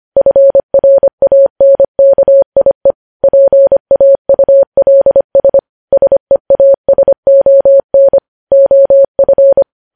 Morse code, heard as a keyed tone, about 25 wpm, 550 Hz. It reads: FRANKIE PAULS SEASON OF
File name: podcast / News headlines-1x-25x25